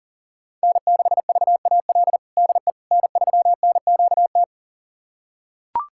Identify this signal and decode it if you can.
Morse code, heard as a keyed tone, about 37 wpm, 700 Hz. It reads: N6VAL DE N3NQT
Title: USA Callsign DE Callsign - 5 Characters - 40wpm